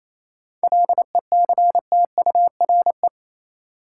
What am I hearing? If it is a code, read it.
Morse code, heard as a keyed tone, about 28 wpm, 700 Hz. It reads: LECTURE
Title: Sentences from Top 1600 Words - New Word Drill Alt - 28wpm